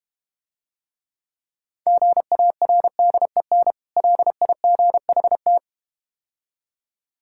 Morse code, heard as a keyed tone, about 31 wpm, 700 Hz. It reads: GARDEN LIGHT